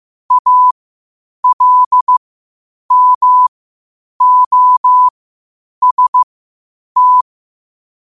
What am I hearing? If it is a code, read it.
Morse code, heard as a keyed tone, about 15 wpm, spaced as for 8 wpm, 1000 Hz. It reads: ALMOST
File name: qotdfiles / 8wpm